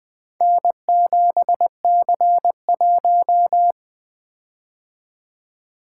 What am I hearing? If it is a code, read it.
Morse code, heard as a keyed tone, about 20 wpm, 700 Hz. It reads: N7C1
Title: Four Letter-Number Combinations - 20wpm